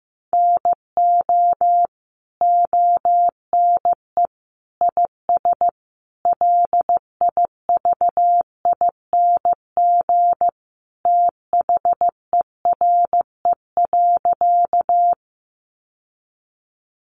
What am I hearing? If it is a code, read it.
Morse code, heard as a keyed tone, about 15 wpm, 700 Hz. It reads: NO ONE IS LIVING THERE.
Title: Sentences from Top 1100 Words 15wpm